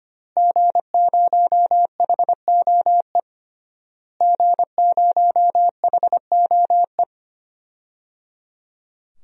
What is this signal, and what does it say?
Morse code, heard as a keyed tone, about 25 wpm, 700 Hz. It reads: G0HOE G0HOE
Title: DXCC-100 Call Signs - 5 Characters - Repeated 3x - 25wpm